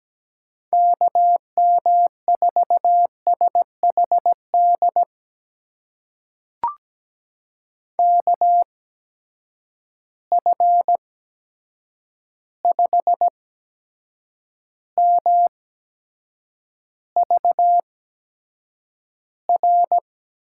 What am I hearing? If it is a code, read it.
Morse code, heard as a keyed tone, about 17 wpm, spaced as for 7 wpm, 700 Hz. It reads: KM4SHD KF5MVR
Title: USA Call Signs - 6 Characters - 8x Character Spacing - 17wpm